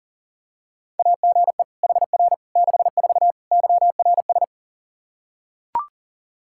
Morse code, heard as a keyed tone, about 40 wpm, 700 Hz. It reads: AGE HR 64 YRS
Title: K1USN SST QSO Practice 40wpm